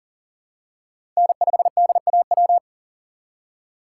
Morse code, heard as a keyed tone, about 40 wpm, 700 Hz. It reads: N5DAW
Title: USA Call Signs - 5 Characters - 4x Character Spacing - 40wpm